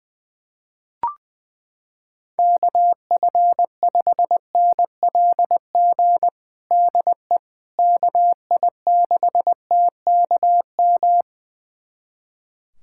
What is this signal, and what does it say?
Morse code, heard as a keyed tone, about 21 wpm, 700 Hz. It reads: KF5NLG DE KI6TKM
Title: USA Callsign DE Callsign - 6 Characters - 20wpm